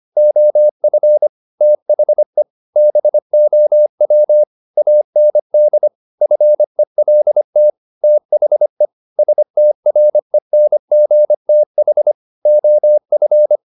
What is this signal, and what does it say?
Morse code, heard as a keyed tone, about 25 wpm, 600 Hz. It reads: OF THE BOW AND FELT THE STRENGTH OF
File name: the old man and the sea part 1 25wpm